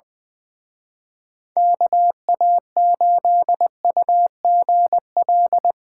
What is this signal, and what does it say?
Morse code, heard as a keyed tone, about 22 wpm, 700 Hz. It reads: KA8UGL
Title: Speed Racing - USA 2x3 Call Signs 20wpm